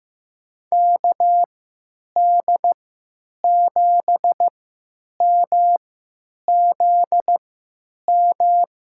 Morse code, heard as a keyed tone, about 15 wpm, spaced as for 9 wpm, 700 Hz. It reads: KD7MZM
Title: USA Call Signs - 3x Character Spacing - 15wpm